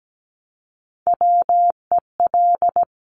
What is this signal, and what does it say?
Morse code, heard as a keyed tone, about 17 wpm, 700 Hz. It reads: WEL
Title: Top Three Letter Combinations in Words 17wpm